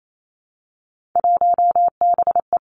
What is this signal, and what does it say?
Morse code, heard as a keyed tone, about 28 wpm, 700 Hz. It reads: 1BE
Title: Three Challenging Letter-Number Combinations 28wpm